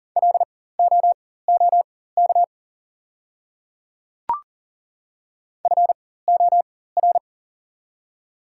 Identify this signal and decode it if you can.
Morse code, heard as a keyed tone, about 40 wpm, spaced as for 19 wpm, 700 Hz. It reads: LOOK FOR
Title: 4x Character Spacing - Top 100 Words - 40wpm